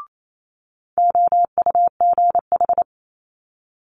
Morse code, heard as a keyed tone, about 28 wpm, 700 Hz. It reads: OUGH
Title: Multi-letter Phonemes Quick Fire 28wpm